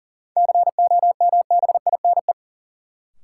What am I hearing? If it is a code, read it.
Morse code, heard as a keyed tone, about 40 wpm, 700 Hz. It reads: COMBINE
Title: Sentences from Top 1600 Words - New Word Drill Alt - 40wpm